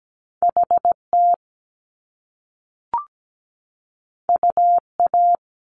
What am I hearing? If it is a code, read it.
Morse code, heard as a keyed tone, about 17 wpm, 700 Hz. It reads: HT UA
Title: Top Two Letter Combinations in Words 17wpm